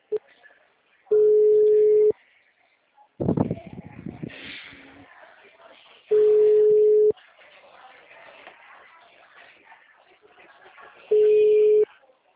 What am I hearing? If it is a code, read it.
Ringback tone (Europe, 425 Hz), with speech.
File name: out- -2032- - - .22846